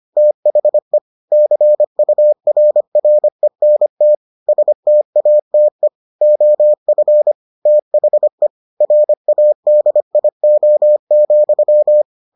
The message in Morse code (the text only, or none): THE CURRENT STATE OF THE RADIO,